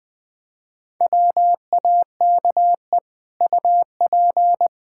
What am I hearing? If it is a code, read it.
Morse code, heard as a keyed tone, about 20 wpm, 700 Hz. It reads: WAKE UP
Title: Easy Sentences with Hints - 20wpm